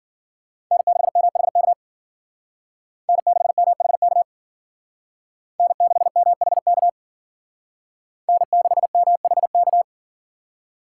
Morse code, heard as a keyed tone, about 51 wpm, 700 Hz. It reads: N6MHK N6MHK N6MHK N6MHK